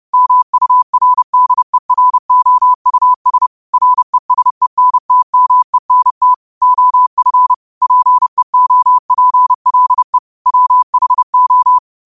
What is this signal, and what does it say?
Morse code, heard as a keyed tone, about 30 wpm, 1000 Hz. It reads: MURDEROUS RESENTMENT OF PEOPLE WHO